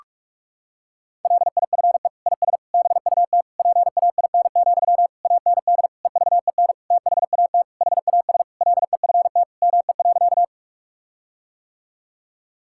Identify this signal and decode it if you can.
Morse code, heard as a keyed tone, about 41 wpm, 700 Hz. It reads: LIFE IS BUT PAIN, AND EVEN THAT HAS LEFT ME.
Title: Sentences from Top 1600 Words 45wpm